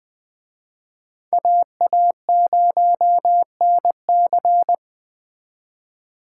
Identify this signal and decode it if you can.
Morse code, heard as a keyed tone, about 24 wpm, 700 Hz. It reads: AA0NC
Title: Speed Racing - USA Call Signs 20wpm